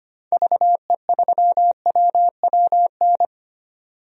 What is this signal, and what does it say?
Morse code, heard as a keyed tone, about 25 wpm, 700 Hz. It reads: VE3WWN
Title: DXCC-100 Call Signs - 6 Characters - 25wpm